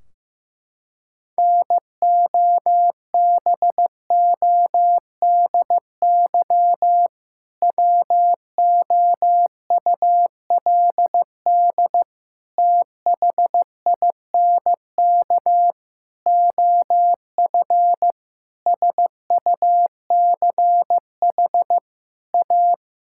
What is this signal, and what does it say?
Morse code, heard as a keyed tone, about 15 wpm, 700 Hz. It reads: NOBODY WOULD THINK OF SUCH A
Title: Sentences from Top 1600 Words 15wpm